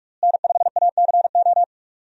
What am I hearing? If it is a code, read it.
Morse code, heard as a keyed tone, about 42 wpm, 700 Hz. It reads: N5ACO
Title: USA Call Signs - 5 Characters - Repeated 3x - 45wpm